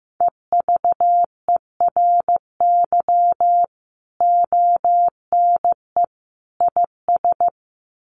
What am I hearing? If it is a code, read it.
Morse code, heard as a keyed tone, about 15 wpm, 700 Hz. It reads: EVERY ONE IS
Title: Sentences from Top 1200 Words 15wpm